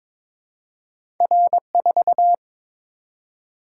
Morse code, heard as a keyed tone, about 22 wpm, 700 Hz. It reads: R4
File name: practice / 22wpm / Two Letter-Number Combinations - 22wpm